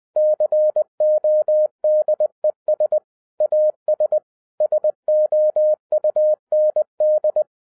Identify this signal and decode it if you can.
Morse code, heard as a keyed tone, about 20 wpm, 610 Hz. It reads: CODES AS SOUND
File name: 2011-10-mp3 / n13-TIPS FROM MY 34 YEARS OF EXPERIENCE 20wpm